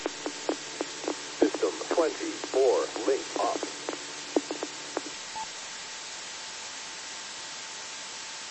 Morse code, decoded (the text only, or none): none